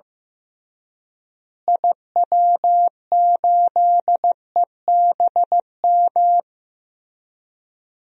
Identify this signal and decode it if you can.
Morse code, heard as a keyed tone, about 15 wpm, 700 Hz. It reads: IW8EBM
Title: DXCC-100 Call Signs - 6 Characters - Repeated 3x - 15wpm